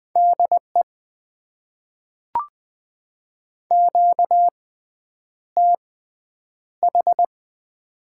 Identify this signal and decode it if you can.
Morse code, heard as a keyed tone, about 20 wpm, spaced as for 9 wpm, 700 Hz. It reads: DE QTH
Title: Top 90 QSO Elements - 6x Character Spacing - 20wpm